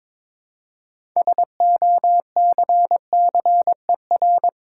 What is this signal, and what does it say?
Morse code, heard as a keyed tone, about 22 wpm, 700 Hz. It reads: SOCCER
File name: Difficult Words with Hints - 22wpm